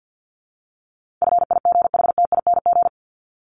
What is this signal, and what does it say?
Morse code, heard as a keyed tone, about 50 wpm, 700 Hz. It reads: FIGHTING
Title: ING Suffix - Top 100 Words with Base Words 50wpm